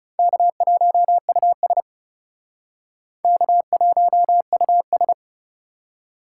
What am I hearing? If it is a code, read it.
Morse code, heard as a keyed tone, about 37 wpm, 700 Hz. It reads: K1US K1US
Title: Speed Racing - USA 1x2 Call Signs 30wpm